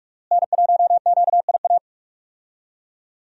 Morse code, heard as a keyed tone, about 42 wpm, 700 Hz. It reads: N1QIA
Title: USA Call Signs - 5 Characters - Repeated 3x - 45wpm